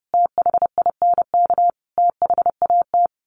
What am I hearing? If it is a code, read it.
Morse code, heard as a keyed tone, about 28 wpm, 700 Hz. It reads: THINK THAT
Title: Sets of 5 Words Encore2 30wpm